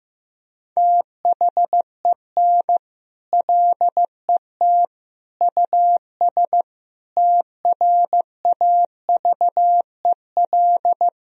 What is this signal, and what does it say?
Morse code, heard as a keyed tone, about 15 wpm, 700 Hz. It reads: THEN LET US TRAVEL
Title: Sentences from Top 1000 Words - Review Part 2 - 15wpm